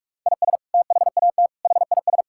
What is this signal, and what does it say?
Morse code, heard as a keyed tone, about 42 wpm, 700 Hz. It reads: IS THAT HIS
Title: Sentences from Top 1000 Words - Review Part 5 - 45wpm